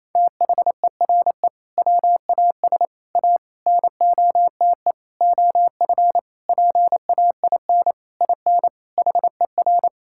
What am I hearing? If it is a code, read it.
Morse code, heard as a keyed tone, about 27 wpm, 700 Hz. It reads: THERE WAS A NOTE OF PAIN IN HER
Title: Sentences from Top 1600 Words 28wpm